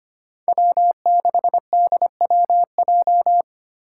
Morse code, heard as a keyed tone, about 25 wpm, 700 Hz. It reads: W6DWJ
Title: Five Challenging Letter-Number Combinations 25wpm